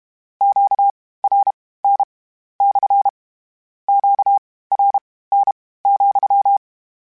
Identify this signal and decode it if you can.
Morse code, heard as a keyed tone, about 32 wpm, spaced as for 18 wpm, 800 Hz. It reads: QRN / QRN,
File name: Q Calls / Q Calls-mp3s / Q CALLS SPSL @30